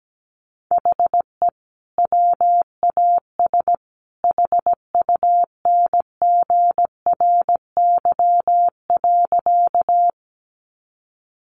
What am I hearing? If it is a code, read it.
Morse code, heard as a keyed tone, about 17 wpm, 700 Hz. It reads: HE WAS HUNGRY.
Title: Sentences from Top 1400 Words 17wpm